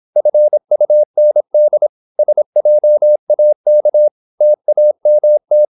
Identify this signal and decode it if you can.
Morse code, heard as a keyed tone, about 26 wpm, 600 Hz. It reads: FUND SJAK TAMT
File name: Korte ord DK 26wpm